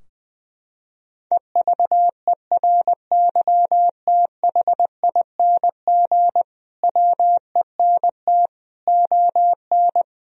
Morse code, heard as a keyed tone, about 20 wpm, 700 Hz. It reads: EVERYTHING WENT ON
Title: Sentences from Top 1000 Words - Review Part 4 - 20wpm